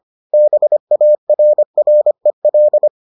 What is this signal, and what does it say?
Morse code, heard as a keyed tone, about 25 wpm, 600 Hz. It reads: BARREL